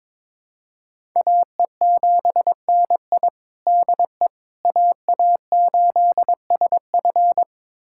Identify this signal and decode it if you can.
Morse code, heard as a keyed tone, about 22 wpm, 700 Hz. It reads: AE7NI DE AA8SF
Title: USA Callsign DE Callsign - 5 Characters - 22wpm